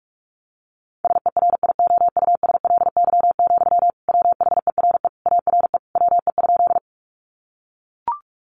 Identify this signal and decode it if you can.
Morse code, heard as a keyed tone, about 44 wpm, 700 Hz. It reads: SERIOUSLY, WHERE ARE WE?
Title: Sentences from Top 1200 Words 45wpm